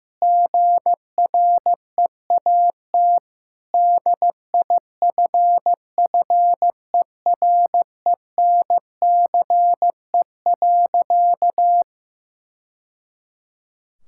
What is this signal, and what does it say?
Morse code, heard as a keyed tone, about 16 wpm, 700 Hz. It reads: GREAT DIFFERENCE.
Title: Sentences from Top 1000 Words - Review Part 5 - 15wpm